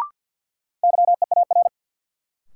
Morse code, heard as a keyed tone, about 53 wpm, 700 Hz. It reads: YEAR
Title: Sentences from Top 200 Words - New Word Drill - 50wpm